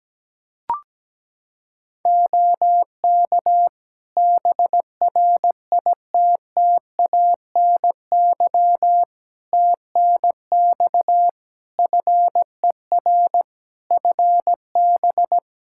Morse code, heard as a keyed tone, about 17 wpm, 700 Hz. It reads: OK BRITTANY TNX FER FB
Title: Practice QSO - Exchange 4 - Ending - 17wpm